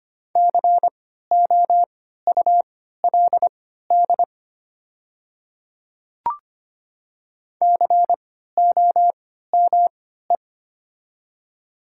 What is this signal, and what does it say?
Morse code, heard as a keyed tone, about 25 wpm, spaced as for 14 wpm, 700 Hz. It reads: COULD COME
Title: 3x Character Spacing - Top 100 Words - 25wpm